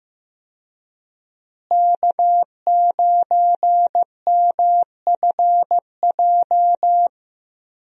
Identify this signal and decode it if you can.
Morse code, heard as a keyed tone, about 15 wpm, 700 Hz. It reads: K9MFJ
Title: USA Call Signs - 5 Characters - Repeated 3x - 15wpm